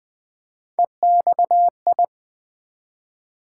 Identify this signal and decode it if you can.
Morse code, heard as a keyed tone, about 20 wpm, 700 Hz. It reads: EXI
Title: Build A Word - 1 Letter At A Time - 5-Letter Words - 20wpm